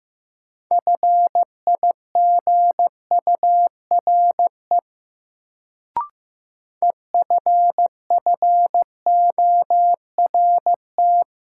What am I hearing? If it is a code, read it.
Morse code, heard as a keyed tone, about 15 wpm, 700 Hz. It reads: FIGURE EFFORT
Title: Sets of 4 Words - New Word Drill - 15wpm